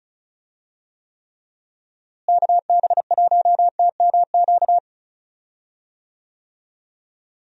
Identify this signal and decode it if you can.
Morse code, heard as a keyed tone, about 36 wpm, 700 Hz. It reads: KB1TMQ
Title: USA Call Signs - 6 Characters - Repeated 2x - 35wpm